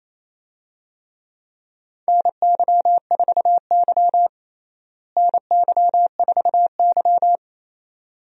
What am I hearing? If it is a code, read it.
Morse code, heard as a keyed tone, about 27 wpm, 700 Hz. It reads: NY4Y NY4Y
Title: USA Call Signs - 4 Characters - Repeated 2x - 28wpm